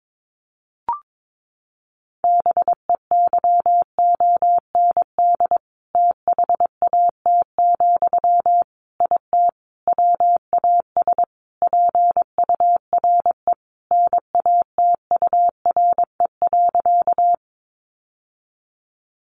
Morse code, heard as a keyed tone, about 22 wpm, 700 Hz. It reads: BEYOND THAT, IT WAS PURE NATURE.
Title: Sentences from Top 1500 Words 22wpm